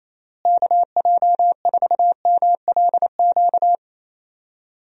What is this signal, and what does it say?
Morse code, heard as a keyed tone, about 28 wpm, 700 Hz. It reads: KJ4MLQ
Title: USA Call Signs - 6 Characters - Repeated 3x - 28wpm